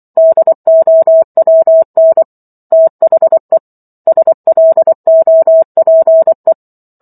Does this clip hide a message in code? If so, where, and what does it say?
Yes: Morse code from 0.2 to 6.5 s, heard as a keyed tone, about 24 wpm, 650 Hz. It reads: DOWN THE SLOPE